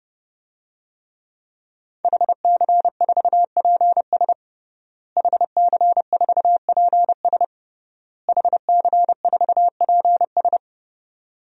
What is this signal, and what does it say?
Morse code, heard as a keyed tone, about 30 wpm, 700 Hz. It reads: HC4PS HC4PS HC4PS
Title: DXCC-100 Call Signs - 5 Characters - Repeated 3x - 30wpm